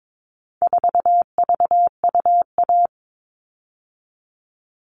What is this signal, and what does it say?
Morse code, heard as a keyed tone, about 22 wpm, 700 Hz. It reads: 4VUA
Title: All Cut Numbers - 4-Digit 22wpm